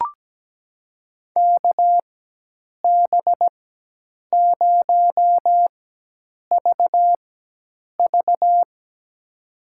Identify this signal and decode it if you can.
Morse code, heard as a keyed tone, about 17 wpm, spaced as for 8 wpm, 700 Hz. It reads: KB0VV
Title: USA Call Signs - 4x Character Spacing - 17wpm